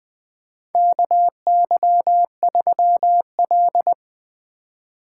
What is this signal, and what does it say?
Morse code, heard as a keyed tone, about 21 wpm, 700 Hz. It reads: KY3L
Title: USA Call Signs - 4 Characters - 6x Character Spacing - 20wpm